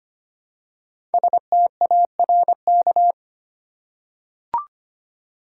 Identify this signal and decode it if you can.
Morse code, heard as a keyed tone, about 25 wpm, 700 Hz. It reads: STARK
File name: 5 Letter Words - First 3 Letters Given - 25wpm